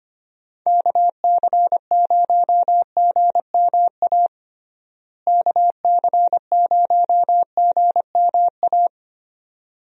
Morse code, heard as a keyed tone, about 25 wpm, 700 Hz. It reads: KC0GMA KC0GMA
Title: USA Call Signs - 6 Characters - Repeated 2x - 25wpm